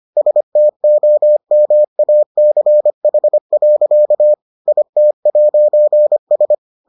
Morse code, heard as a keyed tone, about 25 wpm, 600 Hz. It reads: STOMACH. IT'S